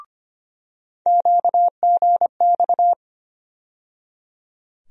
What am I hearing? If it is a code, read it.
Morse code, heard as a keyed tone, about 25 wpm, 700 Hz. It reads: QGX
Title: USA Call Sign Postfixes - 3 Letters 25wpm